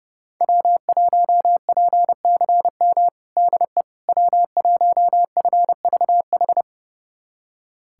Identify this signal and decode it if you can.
Morse code, heard as a keyed tone, about 32 wpm, 700 Hz. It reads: W1PCM DE W1FVH
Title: USA Callsign DE Callsign - 5 Characters - 30wpm